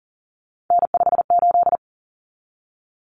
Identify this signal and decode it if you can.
Morse code, heard as a keyed tone, about 40 wpm, 700 Hz. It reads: N58
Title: Common Cut Numbers - 3-Digit 40wpm